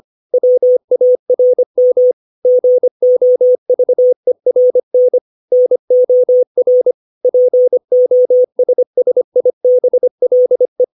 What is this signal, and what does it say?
Morse code, heard as a keyed tone, about 25 wpm, 500 Hz. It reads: WARM GOVERN NOR POSSIBLE
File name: 632 words / 632 Words 25WPM